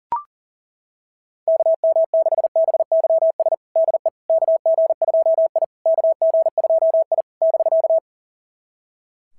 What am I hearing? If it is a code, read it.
Morse code, heard as a keyed tone, about 37 wpm, 660 Hz. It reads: KM6BYS DE KG2I KG2I <BK>
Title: SOTA QSO Practice 40wpm